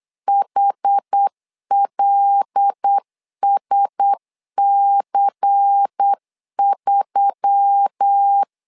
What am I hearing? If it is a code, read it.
Morse code, heard as a keyed tone, about 8 wpm, 800 Hz. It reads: HLSC3
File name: LEZIONI / 5 LBVC / 5 LBVC 8wpm 2935